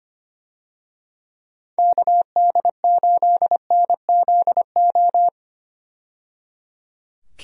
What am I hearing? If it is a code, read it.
Morse code, heard as a keyed tone, about 25 wpm, 700 Hz. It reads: KD8NZO
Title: USA Call Signs - 6 Characters - 25wpm